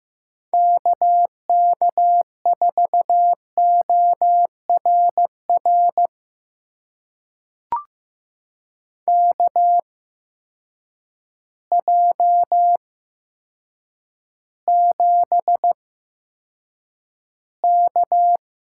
Morse code, heard as a keyed tone, about 15 wpm, spaced as for 7 wpm, 700 Hz. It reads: KK4ORR KJ7K
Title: USA Call Signs - 6 Characters - 8x Character Spacing - 15wpm